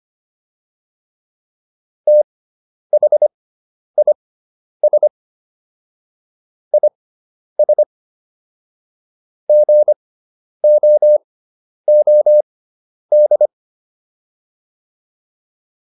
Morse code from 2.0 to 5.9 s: THIS

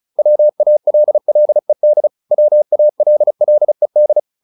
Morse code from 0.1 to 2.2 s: WALLED

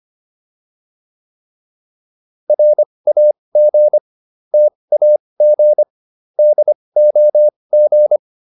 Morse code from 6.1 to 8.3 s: DOG

